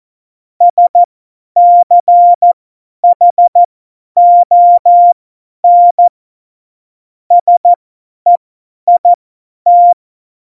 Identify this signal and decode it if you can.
Morse code, heard as a keyed tone, about 14 wpm, spaced as for 10 wpm, 700 Hz. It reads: SCHON SEIT
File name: Morsekurs / Witze / Witz 29. Dezember 2020, 50 F70